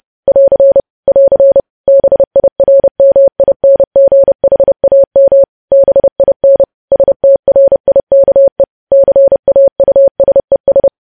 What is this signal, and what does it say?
Morse code, heard as a keyed tone, about 30 wpm, 550 Hz. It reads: <AR> <AR> BIRMINGHAM BIN STRIKE CAUSES
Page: news headlines in morse code at 30 wpm